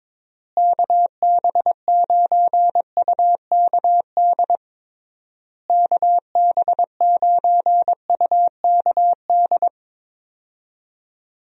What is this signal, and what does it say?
Morse code, heard as a keyed tone, about 22 wpm, 700 Hz. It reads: KB9UKD KB9UKD
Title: USA Call Signs - 6 Characters - Repeated 3x - 22wpm